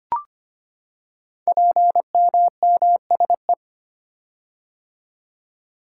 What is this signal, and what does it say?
Morse code, heard as a keyed tone, about 25 wpm, 700 Hz. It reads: PMMSE
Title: Code Groups - 5 Letters - 25wpm